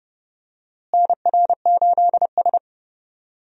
Morse code, heard as a keyed tone, about 29 wpm, 700 Hz. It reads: NR8S
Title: USA Call Signs - 4 Characters - Repeated 2x - 30wpm